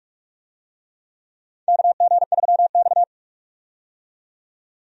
Morse code, heard as a keyed tone, about 46 wpm, 700 Hz. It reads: KG3X
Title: USA Call Signs - 4 Characters - 45wpm